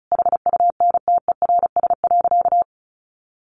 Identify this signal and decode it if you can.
Morse code, heard as a keyed tone, about 34 wpm, 700 Hz. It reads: HUNTERS.
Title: A Sound Of Thunder 35wpm